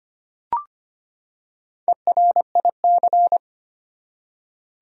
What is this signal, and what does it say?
Morse code, heard as a keyed tone, about 25 wpm, 700 Hz. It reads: ERIC
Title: USA Common Names - Max 4 Letters - 25wpm